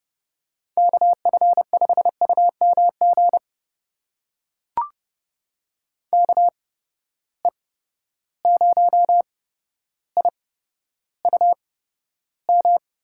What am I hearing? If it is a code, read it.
Morse code, heard as a keyed tone, about 30 wpm, spaced as for 13 wpm, 700 Hz. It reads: KF5UMG KE0IUM
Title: USA Call Signs - 6 Characters - 8x Character Spacing - 30wpm